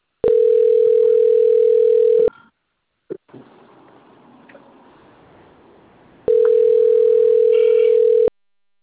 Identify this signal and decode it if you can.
Ringback tone (North America), 440 + 480 Hz.